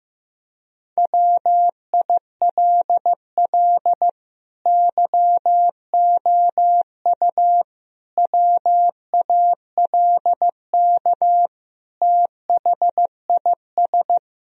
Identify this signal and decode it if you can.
Morse code, heard as a keyed tone, about 15 wpm, 700 Hz. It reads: WILL YOU WALK THIS
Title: Sentences from Top 1000 Words - Review Part 1 - 15wpm